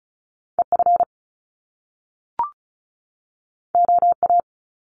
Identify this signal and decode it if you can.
Morse code, heard as a keyed tone, about 36 wpm, 700 Hz. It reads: EF OA
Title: Top Two Letter Combinations in Words 35wpm